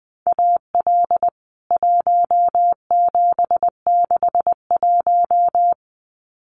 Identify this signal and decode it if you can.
Morse code, heard as a keyed tone, about 20 wpm, 700 Hz. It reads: AL 1761
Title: CWT Exchange 20wpm